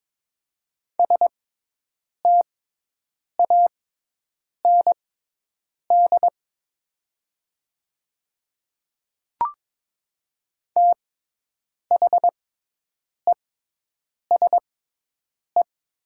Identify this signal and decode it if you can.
Morse code, heard as a keyed tone, about 22 wpm, spaced as for 8 wpm, 700 Hz. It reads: STAND THESE
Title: 6x Character Spacing - Top 300 Words - 22wpm